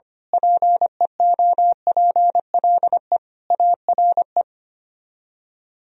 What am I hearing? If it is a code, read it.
Morse code, heard as a keyed tone, about 24 wpm, 700 Hz. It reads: PEOPLE ARE
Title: Speed-Racing - Sets of 5 Words 25wpm